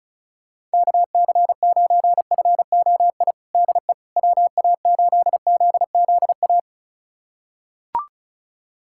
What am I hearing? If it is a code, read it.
Morse code, heard as a keyed tone, about 36 wpm, 700 Hz. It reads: KC9FOI DE WA8ZZA
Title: USA Callsign DE Callsign - 6 Characters - 35wpm